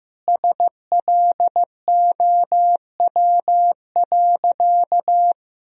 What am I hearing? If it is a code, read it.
Morse code, heard as a keyed tone, about 16 wpm, 700 Hz. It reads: SLOW.
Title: Sentences from Top 400 Words and Abbreviations 15wpm